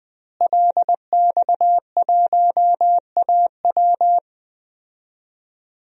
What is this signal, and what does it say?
Morse code, heard as a keyed tone, about 20 wpm, 700 Hz. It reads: LX1AW
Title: DXCC-100 Call Signs - 5 Characters - Repeated 3x - 20wpm